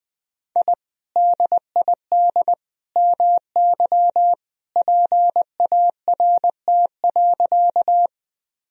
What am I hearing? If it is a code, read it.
Morse code, heard as a keyed tone, about 20 wpm, 700 Hz. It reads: I DID MY PART.
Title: Sentences from Top 1000 Words - Review Part 1 - 20wpm